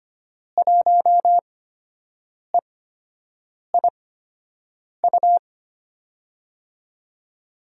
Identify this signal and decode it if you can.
Morse code, heard as a keyed tone, about 25 wpm, spaced as for 7 wpm, 700 Hz. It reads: 1EIU
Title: USA Call Signs - 5 Characters - 8x Character Spacing - 25wpm